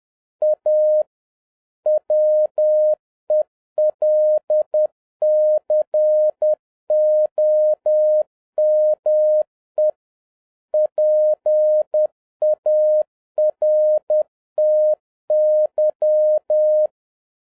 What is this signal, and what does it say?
Morse code, heard as a keyed tone, about 10 wpm, 610 Hz. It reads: A WELCOME PARTY
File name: 2011-07-mp3 / n31 THE 3.11 2 10wpm